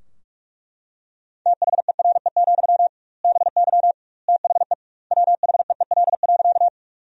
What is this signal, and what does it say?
Morse code, heard as a keyed tone, about 41 wpm, 700 Hz. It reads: THERE, BY THE WHEEL.
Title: 2x Word-Spacing - Sentences from Top 500 Words - 45wpm